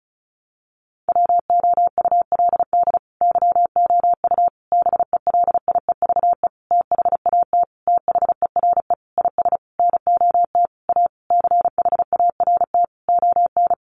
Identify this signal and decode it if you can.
Morse code, heard as a keyed tone, about 34 wpm, 700 Hz. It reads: WOULD YOU BELIEVE THAT THERE IS NOT A CHART ON
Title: Sentences from Top 1300 Words 35wpm